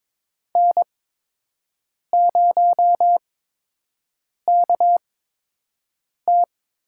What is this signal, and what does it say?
Morse code, heard as a keyed tone, about 22 wpm, spaced as for 6 wpm, 700 Hz. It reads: N0KT
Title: USA Call Signs - 8x Character Spacing - 22wpm